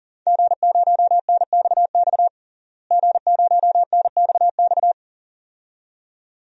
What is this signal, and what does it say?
Morse code, heard as a keyed tone, about 37 wpm, 700 Hz. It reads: G0NXX G0NXX
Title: DXCC-100 Call Signs - 5 Characters - Repeated 2x - 40wpm